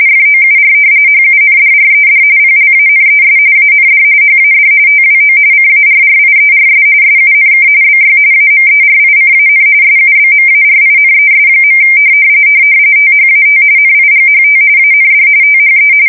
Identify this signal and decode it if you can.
RTTY, about 45.45 baud. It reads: FOUR HUNDRED PLANES AND HAD TWO CARRIERS SUNK BY AMERICAN SUBMARINES. THAT NIGHT AND THE
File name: fsk170 / battleshipnj